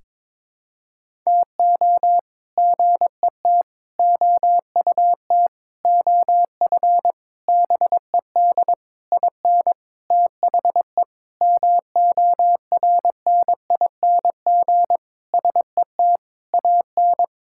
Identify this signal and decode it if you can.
Morse code, heard as a keyed tone, about 22 wpm, 700 Hz. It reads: TO GET OUT OF BED IN THE MORNING SET AN
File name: practice / 22wpm / Sentences with 8-Word Phrases - 22wpm